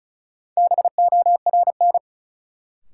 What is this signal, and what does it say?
Morse code, heard as a keyed tone, about 36 wpm, 700 Hz. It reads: BORN
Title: Sentences from Top 1000 Words - New Word Drill - 35wpm